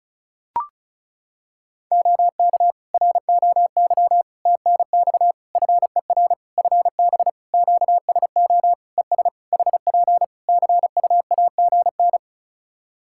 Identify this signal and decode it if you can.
Morse code, heard as a keyed tone, about 34 wpm, 700 Hz. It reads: OK ROY TNX FER FB QSO ES HP CUAGN
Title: Practice QSO - Exchange 4 - Ending - 35wpm